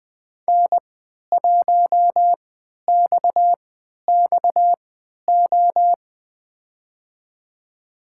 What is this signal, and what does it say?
Morse code, heard as a keyed tone, about 20 wpm, spaced as for 11 wpm, 700 Hz. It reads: N1XXO
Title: USA Call Signs - 5 Characters - 3x Character Spacing - 20wpm